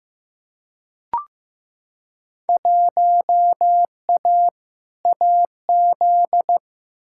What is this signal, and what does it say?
Morse code, heard as a keyed tone, about 15 wpm, 700 Hz. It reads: 1A AZ
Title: Field Day Exchanges - 15wpm